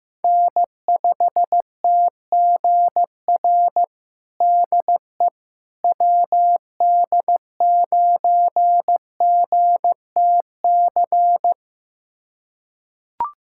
Morse code, heard as a keyed tone, about 15 wpm, 700 Hz. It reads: N5TGR DE WD9GTC
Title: USA Callsign DE Callsign 15wpm